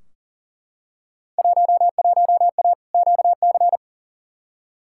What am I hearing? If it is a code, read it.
Morse code, heard as a keyed tone, about 40 wpm, 700 Hz. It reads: 11A QC
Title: Field Day Exchanges Encore - 40wpm